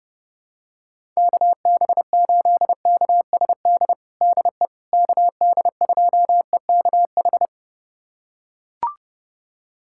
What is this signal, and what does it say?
Morse code, heard as a keyed tone, about 30 wpm, 700 Hz. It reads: KB8KSD DE KD2EKH
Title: USA Callsign DE Callsign 30wpm